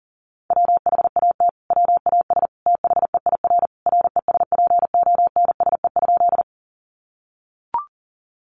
Morse code, heard as a keyed tone, about 40 wpm, 700 Hz. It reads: WHAT WAS THEIR RESPONSE?
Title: Sentences from Top 1400 Words 40wpm